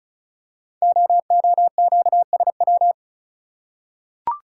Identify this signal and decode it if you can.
Morse code, heard as a keyed tone, about 36 wpm, 690 Hz. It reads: OOQSW
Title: Code Groups - 5 Letters - 35wpm